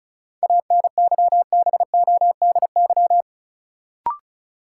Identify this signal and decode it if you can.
Morse code, heard as a keyed tone, about 36 wpm, 700 Hz. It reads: ANYBODY
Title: Sets of 5 Words - New Word Drill - 35wpm